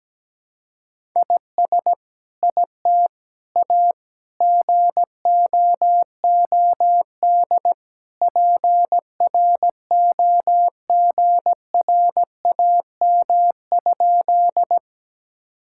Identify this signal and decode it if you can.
Morse code, heard as a keyed tone, about 17 wpm, 700 Hz. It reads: IS IT A GOOD PROGRAM?
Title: Sentences from Top 1000 Words - Review Part 5 - 17wpm